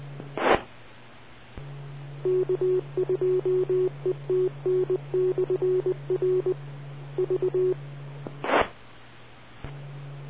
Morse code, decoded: E K2ETN/R V E